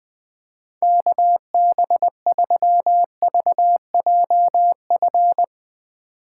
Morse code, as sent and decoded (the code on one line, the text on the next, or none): -.- -... ...-- ...- .--- ..-.
KB3VJF